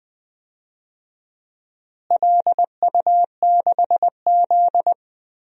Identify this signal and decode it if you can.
Morse code, heard as a keyed tone, about 20 wpm, 700 Hz. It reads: LU6Z
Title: DXCC-100 Call Signs - 4 Characters - Repeated 2x - 20wpm